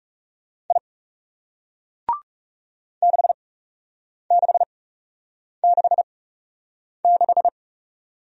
Morse code, heard as a keyed tone, about 38 wpm, spaced as for 8 wpm, 700 Hz. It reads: I 6666
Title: Speed-Racing Single Letter-Number 30wpm